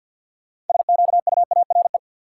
Morse code, heard as a keyed tone, about 50 wpm, 700 Hz. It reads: SQUARE
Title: Sentences from Top 1000 Words - Review Part 7 - New Word Drill - 50wpm